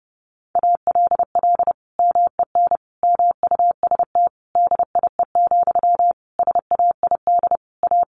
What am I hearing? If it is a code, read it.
Morse code, heard as a keyed tone, about 30 wpm, 700 Hz. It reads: ALL MEN MUST DIE, SAID A
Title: Sentences from Top 1300 Words 30wpm